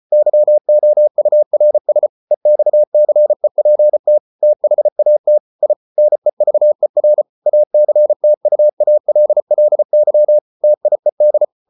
Morse code, heard as a keyed tone, about 34 wpm, 600 Hz. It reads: YOURS EXCEPT THAT I NEVER ACTUALLY TIED